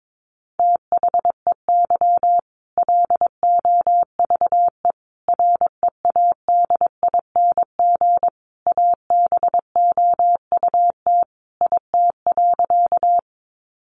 Morse code, heard as a keyed tone, about 22 wpm, 700 Hz. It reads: THEY LOVE READING ABOUT IT.
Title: Sentences from Top 600 Words 22wpm